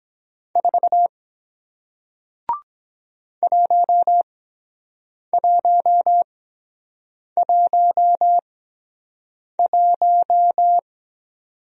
Morse code, heard as a keyed tone, about 23 wpm, spaced as for 7 wpm, 700 Hz. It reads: 4 1111